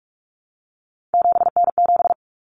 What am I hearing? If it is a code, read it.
Morse code, heard as a keyed tone, about 44 wpm, 700 Hz. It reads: 7N7
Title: Common Cut Numbers - 3-Digit 45wpm